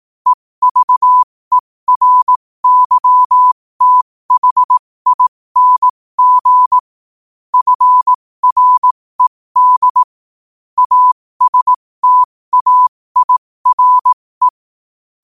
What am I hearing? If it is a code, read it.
Morse code, heard as a keyed tone, about 18 wpm, spaced as for 15 wpm, 1000 Hz. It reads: EVERYTHING FRED ASTAIRE